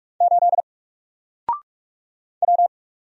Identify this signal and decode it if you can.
Morse code, heard as a keyed tone, about 46 wpm, 700 Hz. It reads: 8 W
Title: Single Character (Letter:Number:Punctuation:Prosign) Alt 45wpm